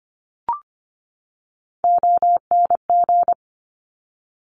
Morse code, heard as a keyed tone, about 25 wpm, 700 Hz. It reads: ONG
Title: Top Three Letter Combinations in Words 25wpm